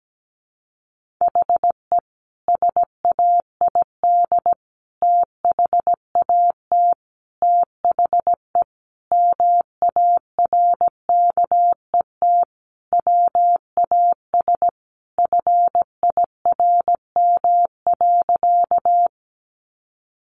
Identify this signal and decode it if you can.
Morse code, heard as a keyed tone, about 17 wpm, 700 Hz. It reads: HE SAID THAT THE MARKET WAS FIRM.
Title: Sentences from Top 1400 Words 17wpm